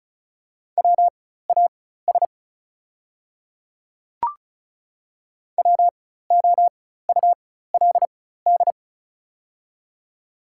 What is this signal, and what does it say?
Morse code, heard as a keyed tone, about 34 wpm, spaced as for 16 wpm, 700 Hz. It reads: WAS WOULD
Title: 4x Character Spacing - Top 300 Words - 35wpm